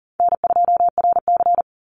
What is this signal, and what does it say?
Morse code, heard as a keyed tone, about 37 wpm, 700 Hz. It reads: N2RC
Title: Easy Call Signs Repeated 3x - 40wpm